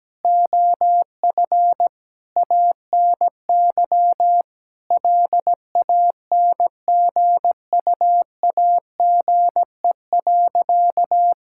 Morse code, heard as a keyed tone, about 17 wpm, 700 Hz. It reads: OF ANY LANGUAGE.
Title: Sentences from Top 1600 Words 17wpm